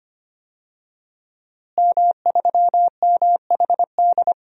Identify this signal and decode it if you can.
Morse code, heard as a keyed tone, about 25 wpm, 700 Hz. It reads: M3MHD
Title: DXCC-100 Call Signs - 5 Characters - Repeated 3x - 25wpm